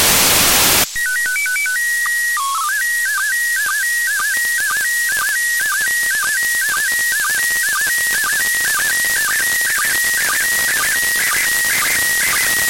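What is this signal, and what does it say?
SSTV (slow-scan television), PD 120.
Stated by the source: SSTV signal from the ISS received on 145.803 MHz.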